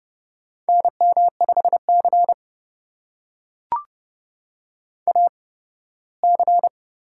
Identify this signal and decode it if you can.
Morse code, heard as a keyed tone, about 32 wpm, spaced as for 18 wpm, 700 Hz. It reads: NM5C AC
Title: USA Call Signs - 4 Characters - 8x Character Spacing - 30wpm